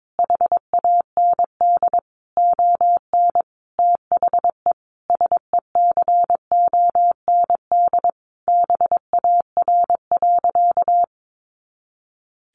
Morse code, heard as a keyed tone, about 22 wpm, 700 Hz. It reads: HAND ON THE SECOND BAR.
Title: Easy Phrases 22wpm